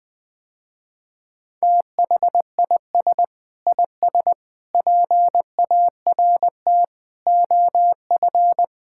Morse code, heard as a keyed tone, about 19 wpm, 700 Hz. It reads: THIS IS PART OF
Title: Sentences with 2-Word Phrases - Spoken Beginning - 20wpm